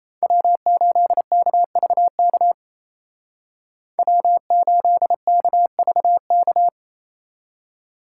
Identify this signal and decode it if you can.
Morse code, heard as a keyed tone, about 36 wpm, 700 Hz. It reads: W8KVK W8KVK